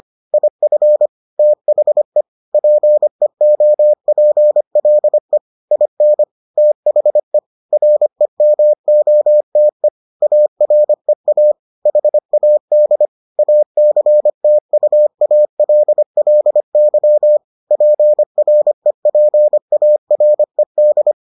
Morse code, heard as a keyed tone, about 25 wpm, 600 Hz. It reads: IF THE PEOPLE IN THE REMOTE AREA HAD ACTUALLY PREPARED